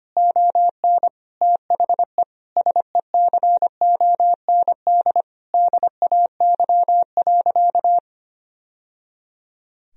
Morse code, heard as a keyed tone, about 25 wpm, 700 Hz. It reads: ON THE SECOND DAY.
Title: Sentences from Top 1000 Words - Review Part 4 - 25wpm